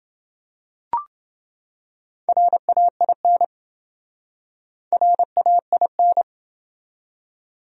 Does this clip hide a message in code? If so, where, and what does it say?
Yes: Morse code from 2.3 to 6.2 s, heard as a keyed tone, about 27 wpm, 700 Hz. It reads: RAIN RAIN